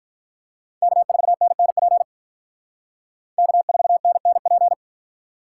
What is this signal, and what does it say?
Morse code, heard as a keyed tone, about 45 wpm, 700 Hz. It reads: K4NNP K4NNP